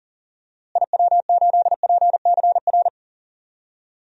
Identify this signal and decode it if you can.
Morse code, heard as a keyed tone, about 40 wpm, 700 Hz. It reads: IW8PCR
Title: DXCC-100 Call Signs - 6 Characters - Repeated 3x - 40wpm